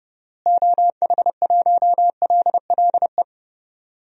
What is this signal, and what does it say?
Morse code, heard as a keyed tone, about 30 wpm, 700 Hz. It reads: OH1LLE
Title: DXCC-100 Call Signs - 6 Characters - Repeated 2x - 30wpm